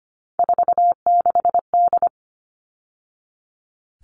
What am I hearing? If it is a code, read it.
Morse code, heard as a keyed tone, about 25 wpm, 700 Hz. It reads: 46D
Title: All Cut Numbers - 3-Digit 25wpm